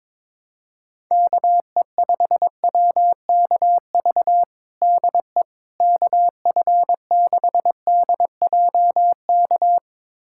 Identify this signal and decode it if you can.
Morse code, heard as a keyed tone, about 21 wpm, 700 Hz. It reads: KE5WKV DE KF6DJK
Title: USA Callsign DE Callsign - 6 Characters - 22wpm